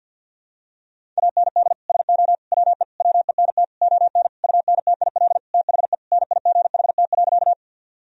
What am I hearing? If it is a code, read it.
Morse code, heard as a keyed tone, about 52 wpm, 700 Hz. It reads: AND SO WE WENT ON UNTIL THE NIGHT.
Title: Sentences from Top 1000 Words - Review Part 1 - 50wpm